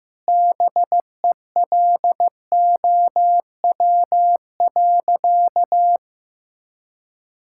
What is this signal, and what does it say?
Morse code, heard as a keyed tone, about 15 wpm, 700 Hz. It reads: BELOW.
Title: Sentences from Top 1000 Words - Review Part 5 - 15wpm